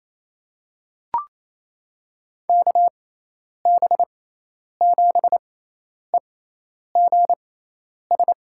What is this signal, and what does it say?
Morse code, heard as a keyed tone, about 28 wpm, spaced as for 10 wpm, 700 Hz. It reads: KB7EGS
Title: USA Call Signs - 6 Characters - 6x Character Spacing - 28wpm